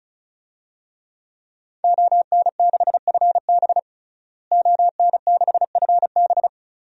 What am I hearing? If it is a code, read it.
Morse code, heard as a keyed tone, about 34 wpm, 700 Hz. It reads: ON6FB ON6FB
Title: DXCC-100 Call Signs - 5 Characters - Repeated 3x - 35wpm